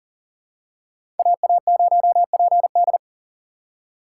Morse code, heard as a keyed tone, about 40 wpm, 700 Hz. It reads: AA0PD
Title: USA Call Signs - 5 Characters - 2x Character Spacing - 40wpm